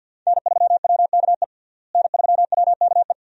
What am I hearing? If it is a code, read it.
Morse code, heard as a keyed tone, about 51 wpm, 700 Hz. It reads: N3WKE N3WKE
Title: USA Call Signs - 5 Characters - Repeated 2x - 50wpm